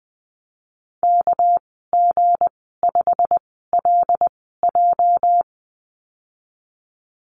Morse code, heard as a keyed tone, about 20 wpm, 700 Hz. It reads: K G 5 L J
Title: 207 Lesson 33g - Letter j - Call Signs 20wpm